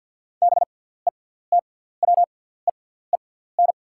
Morse code, heard as a keyed tone, about 48 wpm, spaced as for 17 wpm, 690 Hz. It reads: BETWEEN
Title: 6x Character Spacing - Top 500 Words - 50wpm